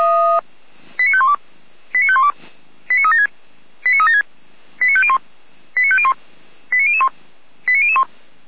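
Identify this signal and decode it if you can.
Five-tone selective call (ZVEI), heard as quick runs of five tones, tones of about 70 ms. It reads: 89512 89512 89376 89376 89501 89501 89001 89001
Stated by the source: ZVEI five-tone call received by SDR, archived by the Feuerwehr Orlamünde fire brigade.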